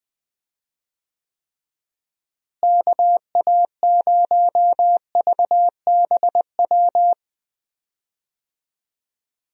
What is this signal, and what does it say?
Morse code, heard as a keyed tone, about 20 wpm, 700 Hz. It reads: KA0VBW
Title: USA Call Signs - 6 Characters - Repeated 2x - 20wpm